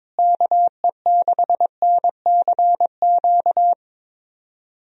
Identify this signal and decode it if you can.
Morse code, heard as a keyed tone, about 22 wpm, 700 Hz. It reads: KE6NCQ
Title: USA Call Signs - 6 Characters - Repeated 2x - 22wpm